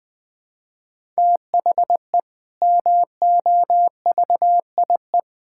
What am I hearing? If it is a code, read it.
Morse code, heard as a keyed tone, about 20 wpm, 700 Hz. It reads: THE MOVIE
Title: Sentences with 6-Word Phrases - 20wpm